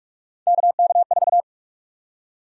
Morse code, heard as a keyed tone, about 46 wpm, 700 Hz. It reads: KK4
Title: USA Call Sign Prefixes - 45wpm